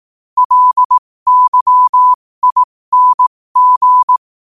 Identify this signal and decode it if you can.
Morse code, heard as a keyed tone, about 18 wpm, spaced as for 15 wpm, 1000 Hz. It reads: LYING
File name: qotdfiles / 15wpm